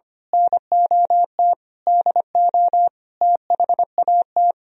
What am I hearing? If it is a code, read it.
Morse code, heard as a keyed tone, about 26 wpm, 700 Hz. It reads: NOT DO THAT
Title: Speed-Racing - Sets of 5 Words 25wpm